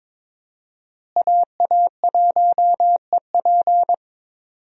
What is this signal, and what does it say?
Morse code, heard as a keyed tone, about 21 wpm, 700 Hz. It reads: AA1EP
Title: USA Call Signs - 5 Characters - Repeated 2x - 22wpm